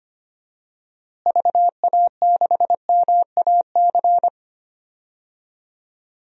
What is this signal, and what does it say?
Morse code, heard as a keyed tone, about 25 wpm, 700 Hz. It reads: VA6MAC
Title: DXCC-100 Call Signs - 6 Characters - Repeated 3x - 25wpm